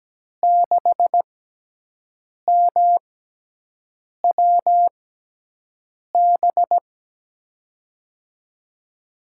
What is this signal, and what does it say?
Morse code, heard as a keyed tone, about 17 wpm, 700 Hz. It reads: 6 M W B
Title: USA Call Signs - 5 Characters - 6x Character Spacing - 17wpm